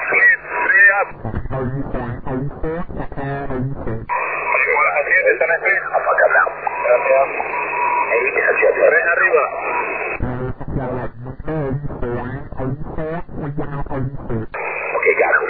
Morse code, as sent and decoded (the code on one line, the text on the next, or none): none